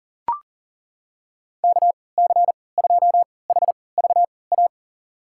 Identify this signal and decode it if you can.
Morse code, heard as a keyed tone, about 40 wpm, spaced as for 23 wpm, 700 Hz. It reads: KC2HVA
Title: USA Call Signs - 6 Characters - 3x Character Spacing - 40wpm